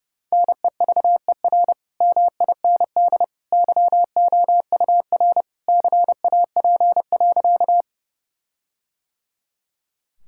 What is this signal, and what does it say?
Morse code, heard as a keyed tone, about 30 wpm, 700 Hz. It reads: NEVER MIND YOUR CAP.
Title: Sentences from Top 1600 Words 30wpm